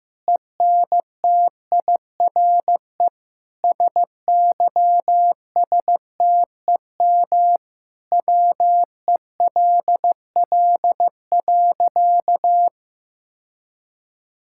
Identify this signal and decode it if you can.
Morse code, heard as a keyed tone, about 15 wpm, 700 Hz. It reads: ENTIRE SYSTEM WELL.
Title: Sentences from Top 1000 Words - Review Part 5 - 15wpm